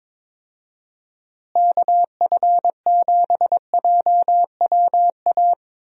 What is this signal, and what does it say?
Morse code, heard as a keyed tone, about 22 wpm, 700 Hz. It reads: KF7JWA
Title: USA Call Signs - 6 Characters - Repeated 2x - 22wpm